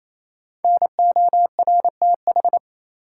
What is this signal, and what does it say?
Morse code, heard as a keyed tone, about 28 wpm, 700 Hz. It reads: NORTH
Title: Sentences from Top 400 Words - New Word Drill - 28wpm